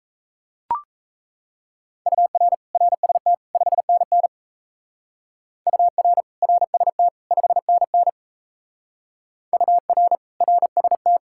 Morse code, heard as a keyed tone, about 37 wpm, 700 Hz. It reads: UR RST 5NN UR RST 5NN UR RST